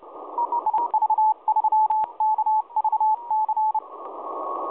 Morse code, heard as a keyed tone, about 29 wpm, 840 Hz. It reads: IV3KVC
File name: 4U2STAYHOME 7016.2kHz